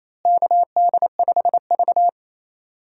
Morse code, heard as a keyed tone, about 27 wpm, 700 Hz. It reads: KD5V
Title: USA Call Signs - 4 Characters - Repeated 3x - 28wpm